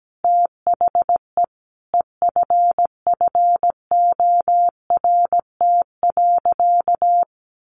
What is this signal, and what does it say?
Morse code, heard as a keyed tone, about 17 wpm, 700 Hz. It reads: THE EFFORT.
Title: Sentences from Top 1200 Words 17wpm